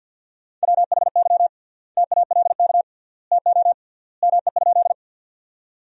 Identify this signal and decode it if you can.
Morse code, heard as a keyed tone, about 50 wpm, 700 Hz. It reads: WHY TALK TO ME?
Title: 3x Word-Spacing - Sentences from Top 400 Words - 50wpm